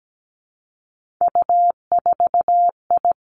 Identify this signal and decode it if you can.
Morse code, heard as a keyed tone, about 17 wpm, 700 Hz. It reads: U4I
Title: Three Challenging Letter-Number Combinations 17wpm